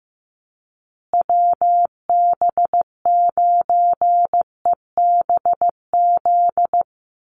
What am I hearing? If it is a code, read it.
Morse code, heard as a keyed tone, about 15 wpm, 700 Hz. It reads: WB9EBZ
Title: USA Call Signs Repeated 3x 15wpm